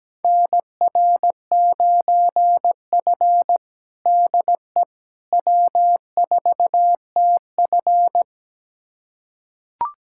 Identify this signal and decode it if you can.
Morse code, heard as a keyed tone, about 17 wpm, 700 Hz. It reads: NR9F DE W4TF
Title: USA Callsign DE Callsign - 4 Characters - 17wpm